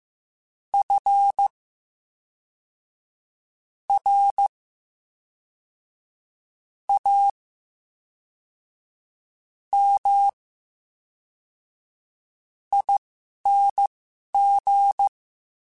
Morse code, heard as a keyed tone, about 15 wpm, spaced as for 11 wpm, 800 Hz. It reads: F R A M ING
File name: Special INGs / Special INGs-mp3s / VerySlowToFaster Ing @10